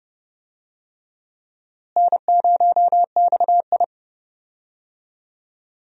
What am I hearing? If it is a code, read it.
Morse code, heard as a keyed tone, about 30 wpm, 700 Hz. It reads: N0XI
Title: USA Call Signs - 4 Characters - Repeated 3x - 30wpm